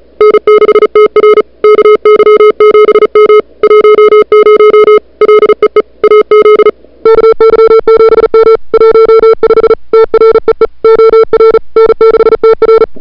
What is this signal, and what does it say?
Morse code, heard as a keyed tone, about 34 wpm, 440 Hz. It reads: N6TR KY7M 10 LEE AZ KY7M 15 TREE OR N6TR
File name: n6tr / Fab2007CWSprintHour1